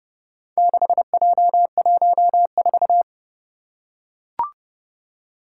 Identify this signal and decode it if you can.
Morse code, heard as a keyed tone, about 30 wpm, 700 Hz. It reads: 6J14
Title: Four Challenging Letter-Number Combinations 30wpm